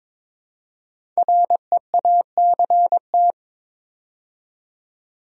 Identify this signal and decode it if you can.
Morse code, heard as a keyed tone, about 23 wpm, 700 Hz. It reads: REACT
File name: 5 Letter Words - First 3 Letters Given - 22wpm